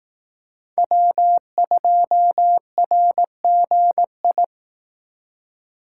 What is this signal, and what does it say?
Morse code, heard as a keyed tone, about 18 wpm, 700 Hz. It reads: W2RGI
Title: Speed Racing - USA 1x3 Call Signs 15wpm